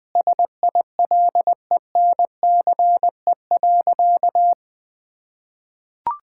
Morse code, heard as a keyed tone, about 20 wpm, 700 Hz. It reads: SILENCE.
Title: Sentences with 3-Word Phrases - 20wpm